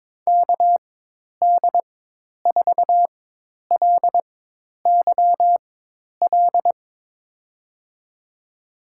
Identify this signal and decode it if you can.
Morse code, heard as a keyed tone, about 22 wpm, spaced as for 10 wpm, 700 Hz. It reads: KD4LYL
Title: USA Call Signs - 4x Character Spacing - 22wpm